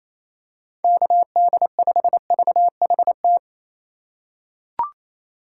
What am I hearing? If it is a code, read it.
Morse code, heard as a keyed tone, about 28 wpm, 700 Hz. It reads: KD5VHT
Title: USA Call Signs - 6 Characters - 3x Character Spacing - 28wpm